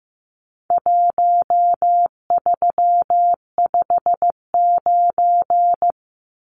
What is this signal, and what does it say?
Morse code, heard as a keyed tone, about 15 wpm, 700 Hz. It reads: 1359
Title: Numbers - Contest Style - 4-Digit 15wpm